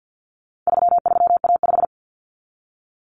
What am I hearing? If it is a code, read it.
Morse code, heard as a keyed tone, about 48 wpm, 700 Hz. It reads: 33A5
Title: Common Cut Numbers - 4-Digit 50wpm